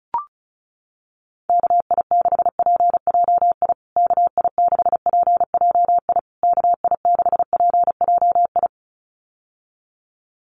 Morse code, heard as a keyed tone, about 34 wpm, 700 Hz. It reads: KI6PJI KI6PJI KI6PJI
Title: USA Call Signs Repeated 3x 35wpm